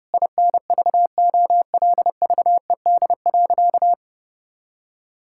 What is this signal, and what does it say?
Morse code, heard as a keyed tone, about 32 wpm, 700 Hz. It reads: INVOLVED.
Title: Sentences from Top 1000 Words - Review Part 5 - 30wpm